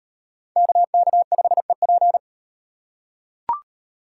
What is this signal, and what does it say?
Morse code, heard as a keyed tone, about 36 wpm, 700 Hz. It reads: KK5EP